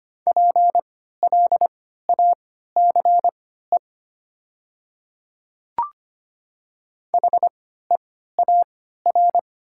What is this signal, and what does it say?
Morse code, heard as a keyed tone, about 25 wpm, spaced as for 14 wpm, 690 Hz. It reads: PLACE HEAR
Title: 3x Character Spacing - Top 500 Words - 25wpm